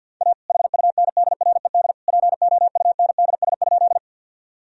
Morse code, heard as a keyed tone, about 47 wpm, 700 Hz. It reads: A HUNDRED POUNDS?
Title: Sentences from Top 1000 Words - Review Part 2 - 50wpm